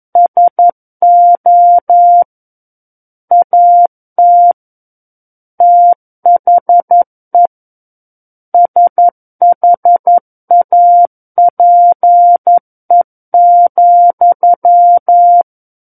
Morse code, heard as a keyed tone, about 11 wpm, 700 Hz. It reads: SO AT THE SHAPE,
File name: Book One / 02-Chapter 2 - 11 WPM